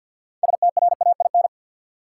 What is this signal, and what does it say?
Morse code, heard as a keyed tone, about 48 wpm, 700 Hz. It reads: STRAIN
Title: Six Letter Words - 50wpm